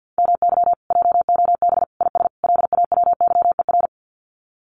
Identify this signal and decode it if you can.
Morse code, heard as a keyed tone, about 50 wpm, 700 Hz. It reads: MY JOB IS LAWYER
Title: Common QSO Phrases Encore 50wpm